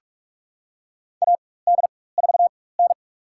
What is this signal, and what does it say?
Morse code, heard as a keyed tone, about 45 wpm, spaced as for 21 wpm, 700 Hz. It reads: AD4N
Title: USA Call Signs - 4x Character Spacing - 45wpm